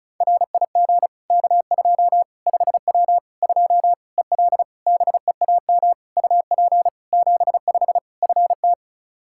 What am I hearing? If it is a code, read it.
Morse code, heard as a keyed tone, about 36 wpm, 700 Hz. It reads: RIG K2 5W 2 EL BEAM UP 75 FT
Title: Practice QSO - Exchange 2 - Info - 35wpm